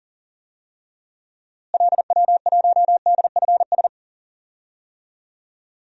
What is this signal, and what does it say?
Morse code, heard as a keyed tone, about 38 wpm, 700 Hz. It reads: LW1DFS
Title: DXCC-100 Call Signs - 6 Characters - Repeated 2x - 40wpm